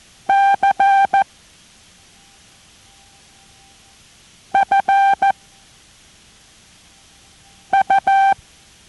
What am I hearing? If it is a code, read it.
Morse code, heard as a keyed tone, about 14 wpm, 790 Hz. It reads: C F U